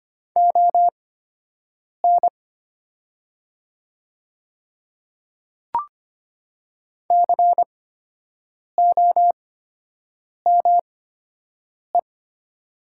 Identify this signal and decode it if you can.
Morse code, heard as a keyed tone, about 25 wpm, spaced as for 7 wpm, 700 Hz. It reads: ON COME
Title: 8x Character Spacing - Top 100 Words - 25wpm